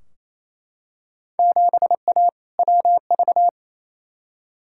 Morse code, heard as a keyed tone, about 28 wpm, 700 Hz. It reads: 7A WV
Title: Field Day Exchanges Encore - 28wpm